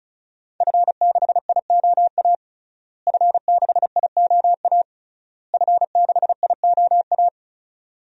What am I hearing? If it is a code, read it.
Morse code, heard as a keyed tone, about 34 wpm, 700 Hz. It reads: F6IOA F6IOA F6IOA
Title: DXCC-100 Call Signs - 5 Characters - Repeated 3x - 35wpm